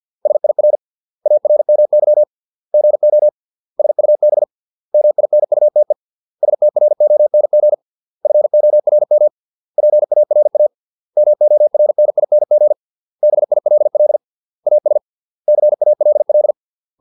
Morse code, heard as a keyed tone, about 50 wpm, 600 Hz. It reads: SIR ARMY GO SUB MINUTE STRONG FORM PARA MORNING BILL AS CALL